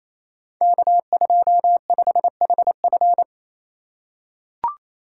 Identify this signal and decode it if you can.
Morse code, heard as a keyed tone, about 29 wpm, 690 Hz. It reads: K25HF
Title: Code Groups - 5 Letter:Numbers - 28wpm